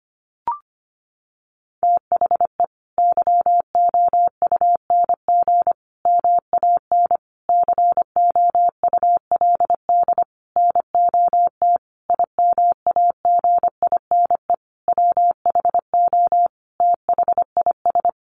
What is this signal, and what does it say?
Morse code, heard as a keyed tone, about 25 wpm, 700 Hz. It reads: THE YOUNG MAN COULD NOT IMAGINE WHO THIS
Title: Sentences from Top 1400 Words 25wpm